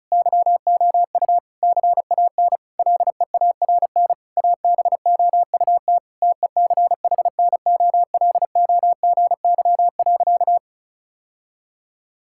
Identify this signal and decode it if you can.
Morse code, heard as a keyed tone, about 36 wpm, 700 Hz. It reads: YOU CAN LEARN ABOUT TECHNOLOGY.
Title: Sentences from Top 1000 Words - Review Part 5 - 35wpm